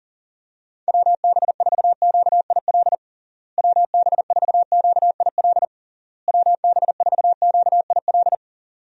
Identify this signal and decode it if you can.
Morse code, heard as a keyed tone, about 37 wpm, 700 Hz. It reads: WB4QIL WB4QIL WB4QIL
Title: USA Call Signs - 6 Characters - Repeated 3x - 40wpm